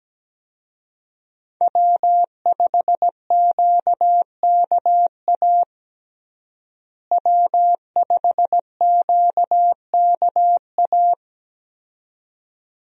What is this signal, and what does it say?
Morse code, heard as a keyed tone, about 17 wpm, 700 Hz. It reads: W5QKA W5QKA
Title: USA Call Signs - 5 Characters - Repeated 2x - 17wpm